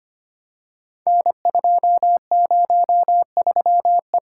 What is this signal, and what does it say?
Morse code, heard as a keyed tone, about 25 wpm, 690 Hz. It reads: N203E
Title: Code Groups - 5 Letter:Numbers - 25wpm